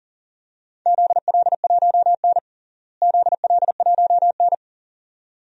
Morse code, heard as a keyed tone, about 40 wpm, 700 Hz. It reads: ZL1N ZL1N
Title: DXCC-100 Call Signs - 4 Characters - Repeated 2x - 40wpm